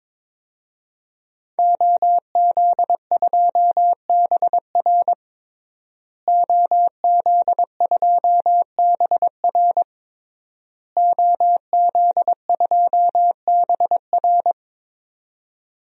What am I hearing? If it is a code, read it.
Morse code, heard as a keyed tone, about 22 wpm, 700 Hz. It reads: OZ2BR OZ2BR OZ2BR
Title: DXCC-100 Call Signs - 5 Characters - Repeated 3x - 22wpm